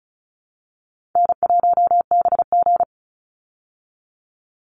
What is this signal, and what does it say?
Morse code, heard as a keyed tone, about 34 wpm, 700 Hz. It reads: N1BG
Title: Easy Call Signs Repeated 2x - 35wpm